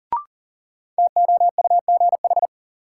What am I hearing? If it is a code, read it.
Morse code, heard as a keyed tone, about 40 wpm, 700 Hz. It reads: TOUGH
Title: Sentences from Top 1200 Words - New Word Drill - 40wpm